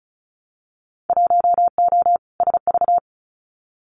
Morse code, heard as a keyed tone, about 36 wpm, 700 Hz. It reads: 1O SV
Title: Winter Field Day Exchanges 35wpm